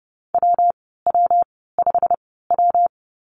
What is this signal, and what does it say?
Morse code, heard as a keyed tone, about 32 wpm, 700 Hz. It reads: W W 5 W
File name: Lesson 19g - Letter w - Call Signs 30wpm